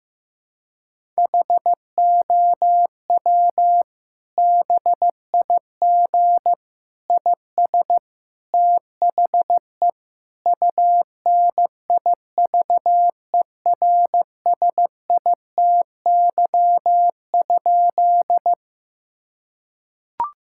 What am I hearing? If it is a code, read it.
Morse code, heard as a keyed tone, about 15 wpm, 700 Hz. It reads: HOW BIG IS THE UNIVERSITY?
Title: Sentences from Top 1000 Words - Review Part 5 - 15wpm